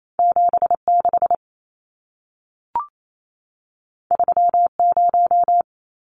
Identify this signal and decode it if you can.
Morse code, heard as a keyed tone, about 28 wpm, 700 Hz. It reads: 76 30
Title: Numbers - 2-Digit 28wpm